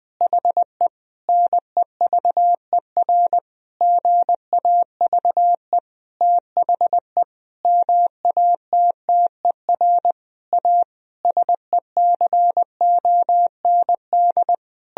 Morse code, heard as a keyed tone, about 20 wpm, 700 Hz. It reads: HE NEVER GAVE THE MATTER A SECOND
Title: Sentences from Top 1000 Words - Review Part 2 - 20wpm